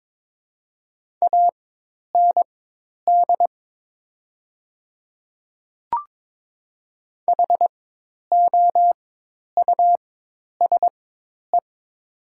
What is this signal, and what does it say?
Morse code, heard as a keyed tone, about 22 wpm, spaced as for 10 wpm, 700 Hz. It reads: AND HOUSE
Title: 4x Character Spacing - Top 300 Words - 22wpm